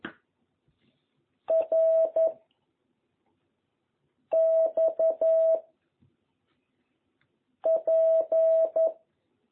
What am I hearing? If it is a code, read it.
Morse code, heard as a keyed tone, about 11 wpm, 660 Hz. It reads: R X P
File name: jh8cbh / 2021cw